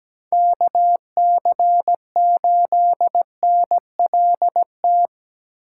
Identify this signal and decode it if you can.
Morse code, heard as a keyed tone, about 17 wpm, 700 Hz. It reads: KC8NLT
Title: USA Call Signs - 6x Character Spacing - 17wpm